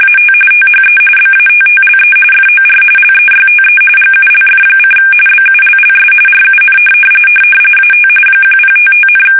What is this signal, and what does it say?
RTTY, about 45.45 baud. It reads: AT THE BATTLE FOR LEYTE GULF: BULLS RUN AT 1055,
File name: fsk850 / battleshipnj